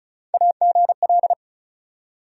Morse code, heard as a keyed tone, about 34 wpm, 700 Hz. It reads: AGL
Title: USA Call Sign Postfixes - 3 Letters 35wpm